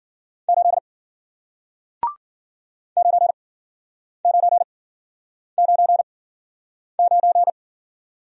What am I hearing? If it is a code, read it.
Morse code, heard as a keyed tone, about 53 wpm, spaced as for 9 wpm, 700 Hz. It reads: 8 9999